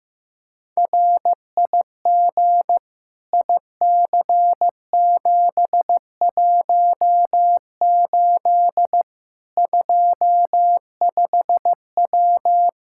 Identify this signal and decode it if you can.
Morse code, heard as a keyed tone, about 16 wpm, 700 Hz. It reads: RIG IC718 25W
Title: Practice QSO - Exchange 2 - Info - 15wpm